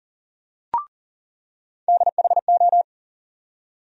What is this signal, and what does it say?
Morse code, heard as a keyed tone, about 38 wpm, 700 Hz. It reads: DHO
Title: USA Call Sign Postfixes - 3 Letters 40wpm